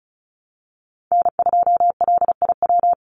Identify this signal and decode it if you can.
Morse code, heard as a keyed tone, about 34 wpm, 700 Hz. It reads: N2LIW
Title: USA Call Signs 35wpm